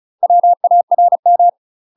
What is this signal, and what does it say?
Morse code, heard as a keyed tone, about 34 wpm, 700 Hz. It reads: WARM
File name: UK MASTER QSOs TRAINING / 35WPM / UK MASTER QSOs 0037